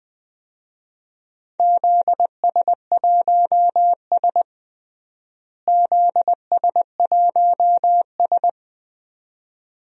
Morse code, heard as a keyed tone, about 19 wpm, 700 Hz. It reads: ZS1S ZS1S
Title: DXCC-100 Call Signs - 4 Characters - Repeated 2x - 20wpm